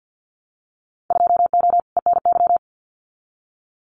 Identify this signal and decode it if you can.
Morse code, heard as a keyed tone, about 50 wpm, 700 Hz. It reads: 2O ENY
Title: Winter Field Day Exchanges Encore 50wpm